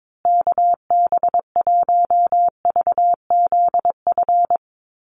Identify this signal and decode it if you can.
Morse code, heard as a keyed tone, about 22 wpm, 700 Hz. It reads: KB1VZF
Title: USA Call Signs Repeated 2x 22wpm